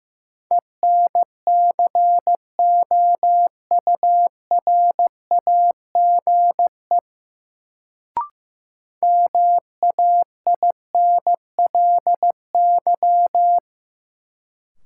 Morse code, heard as a keyed tone, about 15 wpm, 700 Hz. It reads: ENCOURAGE MAINLY
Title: Sentences from Top 1400 Words - New Word Drill Alt - 15wpm